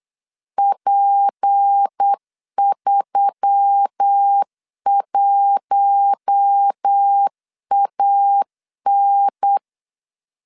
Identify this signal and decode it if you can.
Morse code, heard as a keyed tone, about 8 wpm, 800 Hz. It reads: P31AN
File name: LEZIONI / 6 PKX0 / 6 P 8wpm 3025